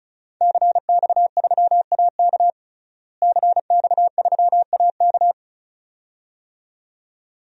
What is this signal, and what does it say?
Morse code, heard as a keyed tone, about 36 wpm, 700 Hz. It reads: CX3AK CX3AK
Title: DXCC-100 Call Signs - 5 Characters - Repeated 2x - 35wpm